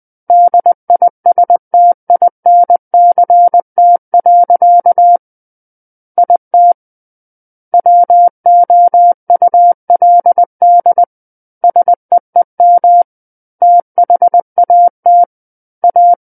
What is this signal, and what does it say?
Morse code, heard as a keyed tone, about 20 wpm, 700 Hz. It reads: DISTINCT. IT WOULD SEEM THAT A
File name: 20 WPM